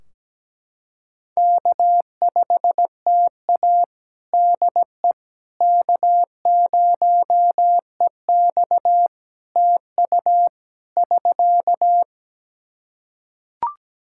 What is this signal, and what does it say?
Morse code, heard as a keyed tone, about 17 wpm, 700 Hz. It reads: K5TA DE K0EX TU <SK>
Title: Practice QSO - Exchange 4 - Ending - 17wpm